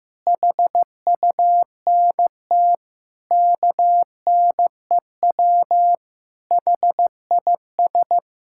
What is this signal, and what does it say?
Morse code, heard as a keyed tone, about 15 wpm, 700 Hz. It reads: HUNT KNEW HIS
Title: Sentences from Top 1600 Words 15wpm